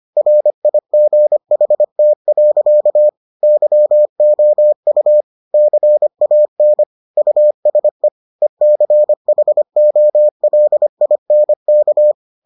Morse code, heard as a keyed tone, about 25 wpm, 600 Hz. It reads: RIGHT. YOU CAN USE ECHOLINK